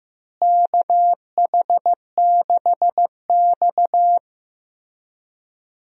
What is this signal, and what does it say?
Morse code, heard as a keyed tone, about 15 wpm, 700 Hz. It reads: KH6X
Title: DXCC-100 Call Signs - 4 Characters - Repeated 2x - 15wpm